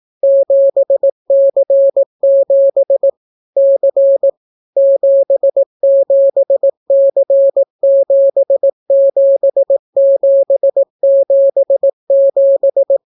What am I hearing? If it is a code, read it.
Morse code, heard as a keyed tone, about 18 wpm, 550 Hz. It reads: 7C7 C 77C77777